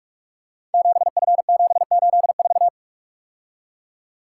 Morse code, heard as a keyed tone, about 44 wpm, 700 Hz. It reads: 7F784
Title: Code Groups - 5 Letter:Numbers - 45wpm